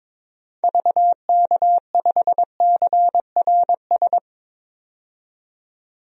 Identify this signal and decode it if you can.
Morse code, heard as a keyed tone, about 22 wpm, 700 Hz. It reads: VK5CRS
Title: DXCC-100 Call Signs - 6 Characters - 22wpm